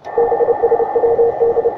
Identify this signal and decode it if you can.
Morse code, heard as a keyed tone, about 29 wpm, 470 Hz, with radio noise. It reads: BSWD
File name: attachments / ua6ypt 15m